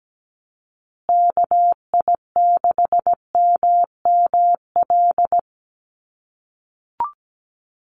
Morse code, heard as a keyed tone, about 17 wpm, 700 Hz. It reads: KI6MML
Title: USA Call Signs Repeated 3x 17wpm